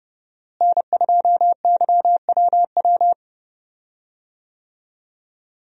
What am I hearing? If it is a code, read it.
Morse code, heard as a keyed tone, about 31 wpm, 700 Hz. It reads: N2YWW
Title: USA Call Signs - 5 Characters - Repeated 2x - 30wpm